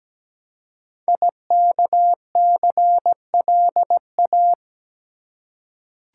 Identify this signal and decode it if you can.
Morse code, heard as a keyed tone, about 17 wpm, 700 Hz. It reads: IKCLA
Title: Code Groups - 5 Letter:Numbers - 17wpm